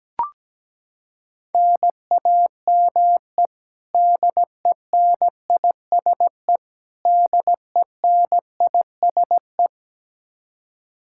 Morse code, heard as a keyed tone, about 17 wpm, 700 Hz. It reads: NAME DENISE DENISE
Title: Practice QSO - Exchange 1 - Beginning - 17wpm